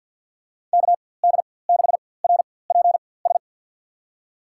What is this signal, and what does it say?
Morse code, heard as a keyed tone, about 51 wpm, spaced as for 23 wpm, 700 Hz. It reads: KD6RPS
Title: USA Call Signs - 4x Character Spacing - 50wpm